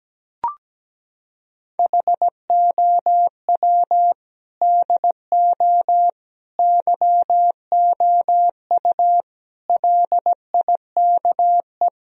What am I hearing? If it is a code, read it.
Morse code, heard as a keyed tone, about 17 wpm, 700 Hz. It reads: HOW DO YOU LIKE
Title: Sentences from Top 1000 Words - Review Part 1 - 17wpm